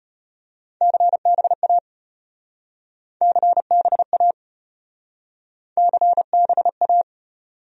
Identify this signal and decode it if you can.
Morse code, heard as a keyed tone, about 34 wpm, 700 Hz. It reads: CBA CBA CBA